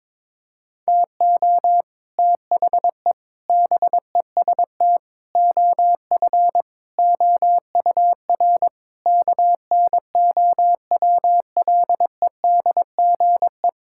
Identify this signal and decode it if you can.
Morse code, heard as a keyed tone, about 22 wpm, 700 Hz. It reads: TO THE BEST OF OUR KNOWLEDGE
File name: Sets of 9 Words - 22wpm